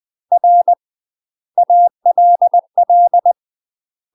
Morse code, heard as a keyed tone, about 20 wpm, 700 Hz. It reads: R ALL
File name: UK MASTER QSOs TRAINING/20WPM